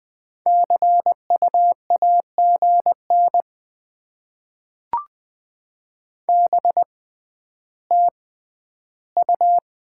Morse code, heard as a keyed tone, about 20 wpm, 700 Hz. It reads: C U A G N B T U
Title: Top 20 QSO Elements - 6x Character Spacing - 20wpm